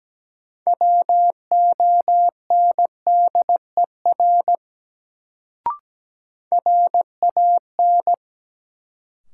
Morse code, heard as a keyed tone, about 17 wpm, 700 Hz. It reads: WONDER RAN
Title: Sentences from Top 500 Words - New Word Drill - 17wpm